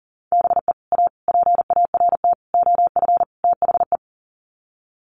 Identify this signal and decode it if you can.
Morse code, heard as a keyed tone, about 40 wpm, 700 Hz. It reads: BE A PART OF THE
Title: Sets of 5 Words Encore 40wpm